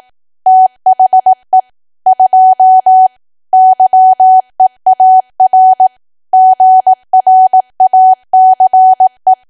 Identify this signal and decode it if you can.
Morse code, heard as a keyed tone, about 18 wpm, 750 Hz. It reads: THE 2 YEAR GRACE
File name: cwa / W1AW-Level2 / 18WPM